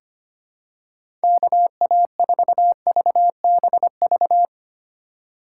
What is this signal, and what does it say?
Morse code, heard as a keyed tone, about 25 wpm, 700 Hz. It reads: KA4VBV
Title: USA Call Signs - 3x Character Spacing - 25wpm